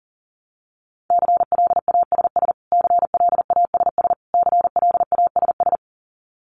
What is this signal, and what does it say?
Morse code, heard as a keyed tone, about 40 wpm, 700 Hz. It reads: CLASS CLASS CLASS
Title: Top 500 Words Repeated 40wpm